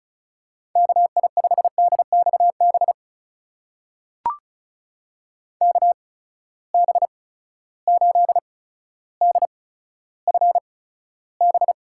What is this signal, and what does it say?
Morse code, heard as a keyed tone, about 36 wpm, spaced as for 15 wpm, 700 Hz. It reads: KI5DXB KB8DFB
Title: USA Call Signs - 6 Characters - 8x Character Spacing - 35wpm